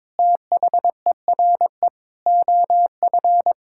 Morse code, heard as a keyed tone, about 21 wpm, 700 Hz. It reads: THERE OF
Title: Sentences from Top 1000 Words - Review Part 6 - 22wpm